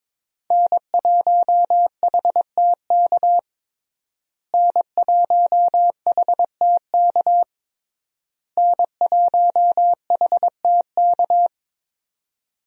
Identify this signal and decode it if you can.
Morse code, heard as a keyed tone, about 22 wpm, 690 Hz. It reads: N1HTK N1HTK N1HTK
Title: USA Call Signs - 5 Characters - Repeated 3x - 22wpm